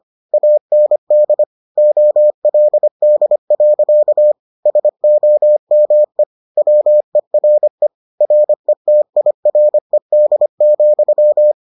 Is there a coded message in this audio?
Yes: Morse code from 0.3 to 11.5 s, heard as a keyed tone, about 25 wpm, 600 Hz. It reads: AND OLD. SOME WERE RETIRED,